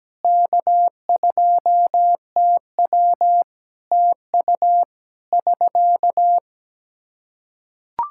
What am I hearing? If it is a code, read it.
Morse code, heard as a keyed tone, about 17 wpm, 700 Hz. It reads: K2TW TU <SK>
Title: Practice QSO - Exchange 4 - Ending - 17wpm